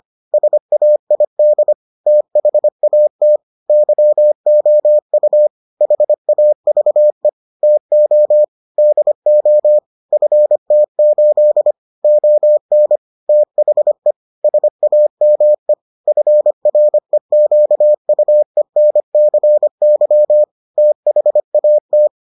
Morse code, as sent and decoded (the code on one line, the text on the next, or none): ... .- .. -.. / - .... .- - / -.-- --- ..- / .... .- ...- . / - --- / -.. --- / ..-. - ---.. / --- -. / - .... / ... .- -- . / ..-. .-. --.- ..- . -. -.-. -.-- / - .... .- -
SAID THAT YOU HAVE TO DO FT8 ON THE SAME FREQUENCY THAT